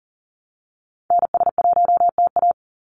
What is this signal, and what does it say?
Morse code, heard as a keyed tone, about 40 wpm, 700 Hz. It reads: NS1TA
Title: NA SOTA Activator Callsigns 40wpm